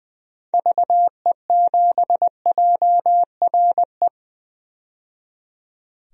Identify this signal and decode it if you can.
Morse code, heard as a keyed tone, about 20 wpm, 700 Hz. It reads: VE7JRE
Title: DXCC-100 Call Signs - 6 Characters - 20wpm